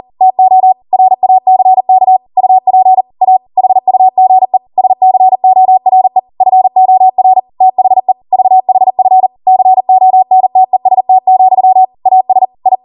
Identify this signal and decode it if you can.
Morse code, heard as a keyed tone, about 40 wpm, 750 Hz. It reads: TO RACK UP A HUGE SCORE FOR THE VHF CONTEST, AS I